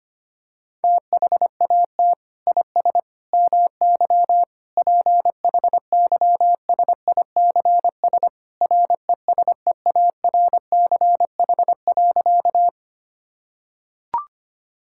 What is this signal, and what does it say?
Morse code, heard as a keyed tone, about 25 wpm, 700 Hz. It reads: THAT IS MY PHYSICS RESEARCH.
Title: Sentences from Top 1000 Words - Review Part 5 - 25wpm